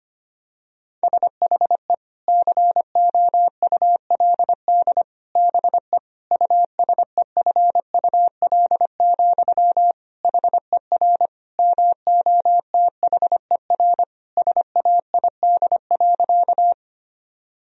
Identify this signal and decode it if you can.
Morse code, heard as a keyed tone, about 25 wpm, 700 Hz. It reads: SHE COULD BE USEFUL, HER MOTHER SAID.
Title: Sentences from Top 1000 Words - Review Part 4 - 25wpm